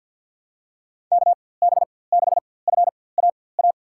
Morse code, heard as a keyed tone, about 51 wpm, spaced as for 23 wpm, 700 Hz. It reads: KB6FAA
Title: USA Call Signs - 6 Characters - 4x Character Spacing - 50wpm